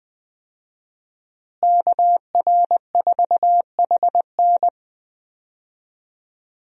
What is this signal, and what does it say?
Morse code, heard as a keyed tone, about 20 wpm, 700 Hz. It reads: KR4HN
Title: USA Call Signs - 5 Characters - Repeated 3x - 20wpm